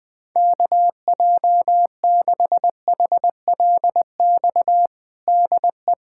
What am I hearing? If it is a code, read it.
Morse code, heard as a keyed tone, about 20 wpm, 700 Hz. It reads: KJ6HLX DE
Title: USA Callsign DE Callsign - 6 Characters - 20wpm